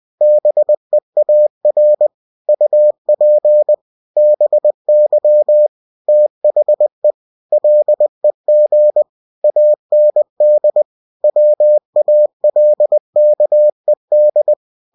Morse code, heard as a keyed tone, about 20 wpm, 600 Hz. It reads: BEAR UP BY THE LEG AND WALKED
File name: cw-books / winnie-the-pooh / winnie-20wpm-page-0110